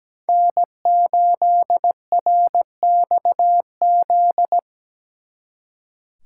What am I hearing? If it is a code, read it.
Morse code, heard as a keyed tone, about 20 wpm, 700 Hz. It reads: N8RXZ
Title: Speed Racing - USA 1x3 Call Signs 17wpm